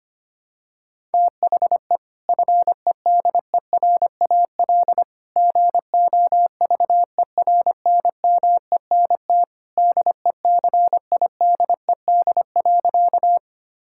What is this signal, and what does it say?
Morse code, heard as a keyed tone, about 25 wpm, 700 Hz. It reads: THE FEDERAL GOVERNMENT DECIDED.
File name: practice / 25wpm / Sentences with 3-Word Phrases - 25wpm